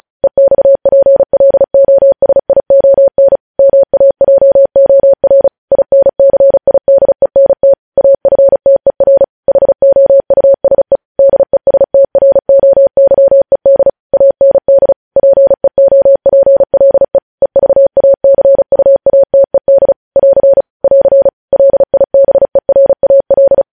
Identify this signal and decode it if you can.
Morse code, heard as a keyed tone, about 36 wpm, 550 Hz. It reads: EXPLOSION MAJOR INCIDENT AFTER HOUSE DESTROYED AND PEOPLE EVACUATED <AR> <AR> LIBERAL
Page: News Headlines in Morse Code at 35 WPM